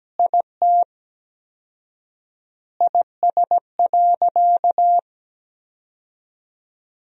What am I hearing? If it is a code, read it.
Morse code, heard as a keyed tone, about 17 wpm, 700 Hz. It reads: IT IS.
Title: 4x Word-Spacing - Sentences from Top 100 Words - 17wpm